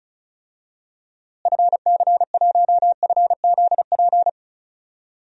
Morse code, heard as a keyed tone, about 36 wpm, 700 Hz. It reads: FC1FZP
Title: DXCC-100 Call Signs - 6 Characters - Repeated 3x - 35wpm